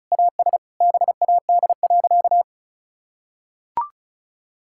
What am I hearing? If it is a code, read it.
Morse code, heard as a keyed tone, about 34 wpm, 700 Hz. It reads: AS BAD.
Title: Sentences from Top 1000 Words - Review Part 6 - 35wpm